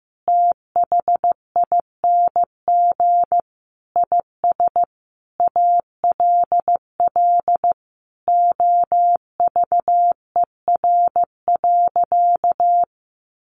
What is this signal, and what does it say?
Morse code, heard as a keyed tone, about 14 wpm, 700 Hz. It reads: THING IS ALL OVER.
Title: Sentences from Top 1300 Words 15wpm